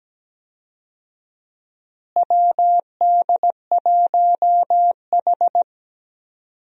Morse code, heard as a keyed tone, about 17 wpm, 700 Hz. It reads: WD1H
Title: USA Call Signs - 4 Characters - Repeated 2x - 17wpm